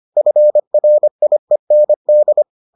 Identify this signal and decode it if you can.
Morse code, heard as a keyed tone, about 25 wpm, 600 Hz. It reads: FRIEND